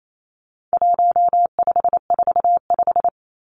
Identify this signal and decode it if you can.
Morse code, heard as a keyed tone, about 28 wpm, 700 Hz. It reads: 1545
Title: Numbers - 4-Digit 28wpm